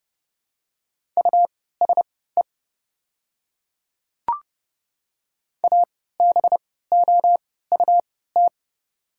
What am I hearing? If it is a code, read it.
Morse code, heard as a keyed tone, about 30 wpm, spaced as for 17 wpm, 690 Hz. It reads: USE ABOUT
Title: 3x Character Spacing - Top 100 Words - 30wpm